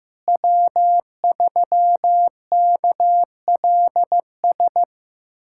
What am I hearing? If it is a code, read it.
Morse code, heard as a keyed tone, about 15 wpm, 700 Hz. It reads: W3KLS
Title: USA Callsign DE Callsign - 5 Characters - 15wpm